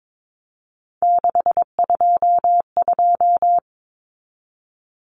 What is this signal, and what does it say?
Morse code, heard as a keyed tone, about 22 wpm, 700 Hz. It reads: 622
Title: Common Cut Numbers - 3-Digit 22wpm